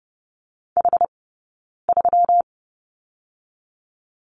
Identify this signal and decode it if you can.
Morse code, heard as a keyed tone, about 30 wpm, 700 Hz. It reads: H 3
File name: Lesson 20d - Number 3 - Two Characters 30wpm